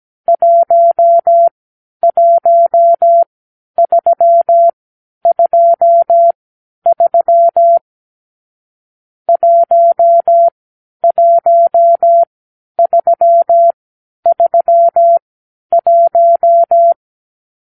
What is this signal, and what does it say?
Morse code, heard as a keyed tone, about 17 wpm, spaced as for 10 wpm, 680 Hz. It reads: 11323 11331
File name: k7qo / Lesson 44 (1-3)